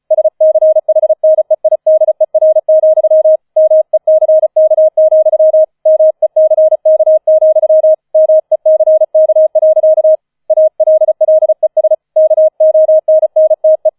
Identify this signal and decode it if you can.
Morse code, heard as a keyed tone, about 36 wpm, 630 Hz. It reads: SCHNEIDER, MECK, MECK, MECK. ALLES KONNTE